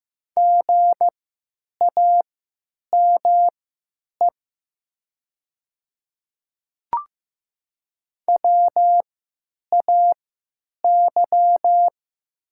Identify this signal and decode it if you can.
Morse code, heard as a keyed tone, about 15 wpm, spaced as for 8 wpm, 700 Hz. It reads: GAME WAY
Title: 3x Character Spacing - Top 500 Words - 15wpm